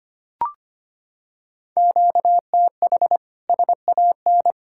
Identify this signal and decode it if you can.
Morse code, heard as a keyed tone, about 25 wpm, 700 Hz. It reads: QTH SAN
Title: Practice QSO - Exchange 1 - Beginning - 25wpm